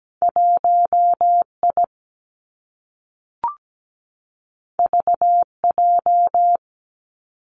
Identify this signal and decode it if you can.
Morse code, heard as a keyed tone, about 17 wpm, 700 Hz. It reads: 1I VJ
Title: Two Challenging Letter-Number Combinations 17wpm